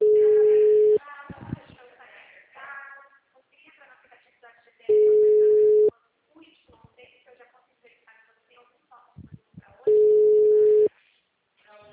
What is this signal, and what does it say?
Ringback tone (Europe, 425 Hz), with speech.
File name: out- -2009